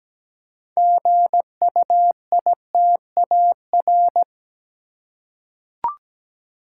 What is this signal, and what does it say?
Morse code, heard as a keyed tone, about 17 wpm, 700 Hz. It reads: GUITAR
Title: Six Letter Words - 17wpm